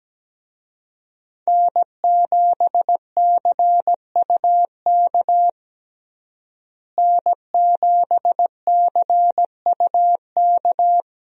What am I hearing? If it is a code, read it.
Morse code, heard as a keyed tone, about 17 wpm, 700 Hz. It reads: N7CUK N7CUK
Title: USA Call Signs - 5 Characters - Repeated 2x - 17wpm